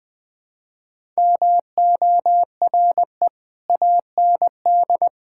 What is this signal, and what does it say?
Morse code, heard as a keyed tone, about 21 wpm, 700 Hz. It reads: MORE AND
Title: Build A Phrase - 3-Word Phrases - 20wpm